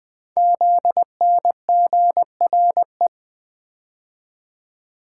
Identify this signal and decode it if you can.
Morse code, heard as a keyed tone, about 20 wpm, 700 Hz. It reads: ZNGRE
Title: Code Groups - 5 Letters - 20wpm